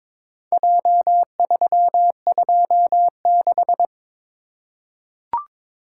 Morse code, heard as a keyed tone, about 22 wpm, 700 Hz. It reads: J326
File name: practice / 22wpm / Four Letter-Number Combinations - 22wpm